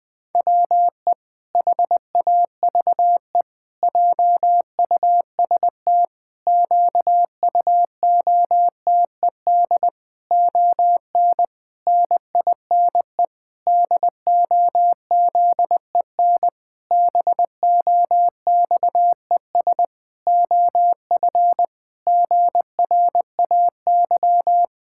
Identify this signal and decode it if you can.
Morse code, heard as a keyed tone, about 21 wpm, 700 Hz. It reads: WE HAVE JUST QUOTED ON NINE DOZEN BOXES OF GRAY
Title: Pangram Challenge v2 - 20wpm